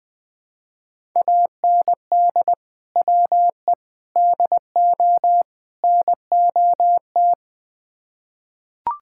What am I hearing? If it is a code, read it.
Morse code, heard as a keyed tone, about 20 wpm, 700 Hz. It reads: AND WE DO NOT
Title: Build A Phrase - 4-Word Phrases - 20wpm